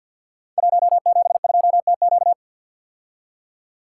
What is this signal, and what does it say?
Morse code, heard as a keyed tone, about 48 wpm, 700 Hz. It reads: 172TQ
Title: Code Groups - 5 Letter:Numbers - 50wpm